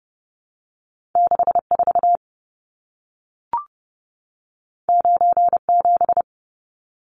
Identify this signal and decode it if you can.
Morse code, heard as a keyed tone, about 30 wpm, 700 Hz. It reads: 64 97
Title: Numbers - 2-Digit 30wpm